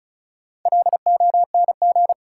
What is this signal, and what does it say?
Morse code, heard as a keyed tone, about 36 wpm, 700 Hz. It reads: LONG
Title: Sets of 2 Words - New Word Drill - 35wpm